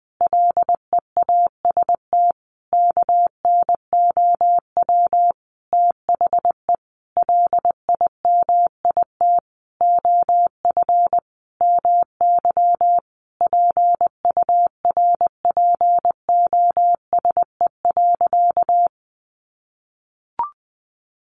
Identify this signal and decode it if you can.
Morse code, heard as a keyed tone, about 20 wpm, 700 Hz. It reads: LEAST KNOW THE LIMIT OF MY PURPOSE.
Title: Sentences from Top 1500 Words 20wpm